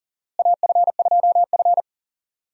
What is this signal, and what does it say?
Morse code, heard as a keyed tone, about 40 wpm, 700 Hz. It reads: AF2F
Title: USA Call Signs - 4 Characters - Repeated 3x - 40wpm